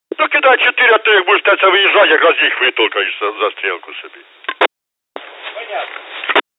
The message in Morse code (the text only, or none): none